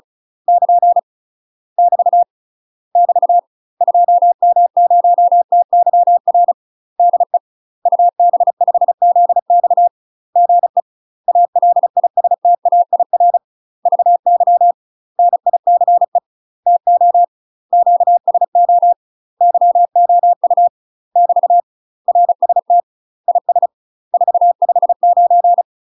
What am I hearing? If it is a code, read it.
Morse code, heard as a keyed tone, about 36 wpm, 700 Hz. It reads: <KN> <BT> <BT> 2M0TYR DE UB5ZX GE ALISTAIR VY NICE TO QSO YOU <BT> RST IS 459
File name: UK MASTER QSOs TRAINING / 35WPM / UK MASTER QSOs 0002